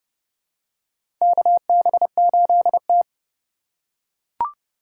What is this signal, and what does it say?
Morse code, heard as a keyed tone, about 30 wpm, 690 Hz. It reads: KB8T
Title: USA Call Signs - 4 Characters - 2x Character Spacing - 30wpm